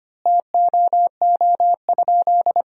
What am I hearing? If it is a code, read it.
Morse code, heard as a keyed tone, about 25 wpm, 700 Hz. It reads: TOO?
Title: 3x Word-Spacing - Sentences from Top 500 Words - 25wpm